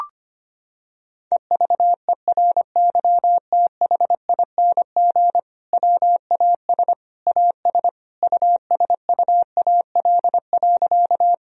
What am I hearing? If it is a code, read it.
Morse code, heard as a keyed tone, about 25 wpm, 700 Hz. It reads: EVERYTHING WAS AS USUAL.
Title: Sentences from Top 1000 Words - Review Part 2 - 25wpm